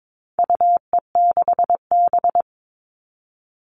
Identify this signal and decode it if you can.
Morse code, heard as a keyed tone, about 22 wpm, 700 Hz. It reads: UE6B
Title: All Cut Numbers - 4-Digit 22wpm